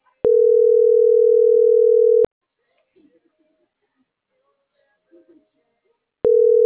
Ringback tone (North America), 440 + 480 Hz.